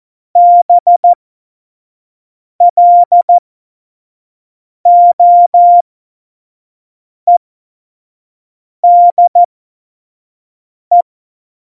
Morse code, heard as a keyed tone, about 14 wpm, spaced as for 5 wpm, 700 Hz. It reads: BLOEDE
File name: Morsekurs / Witze / Witz 11. Juni 2019, 25 F70